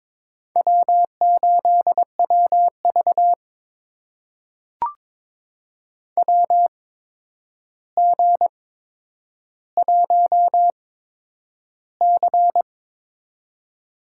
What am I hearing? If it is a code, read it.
Morse code, heard as a keyed tone, about 22 wpm, spaced as for 9 wpm, 700 Hz. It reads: W8WV WG1C
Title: USA Call Signs - 4 Characters - 8x Character Spacing - 22wpm